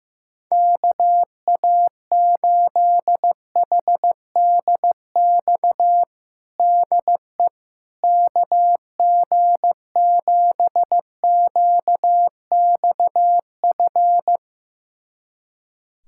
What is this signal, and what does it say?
Morse code, heard as a keyed tone, about 15 wpm, 700 Hz. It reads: KA8HDX DE KG7QXF
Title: USA Callsign DE Callsign - 6 Characters - 15wpm